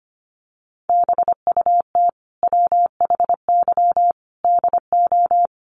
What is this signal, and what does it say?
Morse code, heard as a keyed tone, about 25 wpm, 700 Hz. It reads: BUT WHY DO
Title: Sentences from Top 200 Words 25wpm